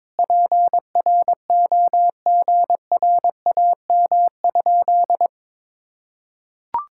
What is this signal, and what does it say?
Morse code, heard as a keyed tone, about 22 wpm, 700 Hz. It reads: PROGRAM?
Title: Sentences from Top 1000 Words - Review Part 5 - 22wpm